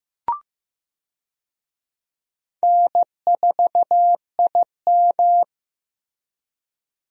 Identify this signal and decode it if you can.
Morse code, heard as a keyed tone, about 16 wpm, 700 Hz. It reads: N4IM
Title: USA Call Signs - 4 Characters - Repeated 2x - 15wpm